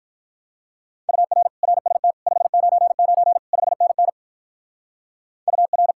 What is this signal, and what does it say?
Morse code, heard as a keyed tone, about 43 wpm, 700 Hz. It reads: UR RST 599 5NN UR
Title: Speed-Racing Common QSO Phrases - 35wpm